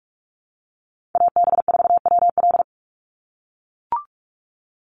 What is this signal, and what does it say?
Morse code, heard as a keyed tone, about 44 wpm, 690 Hz. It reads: AB4WL
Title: NA SOTA Chaser Callsigns 45wpm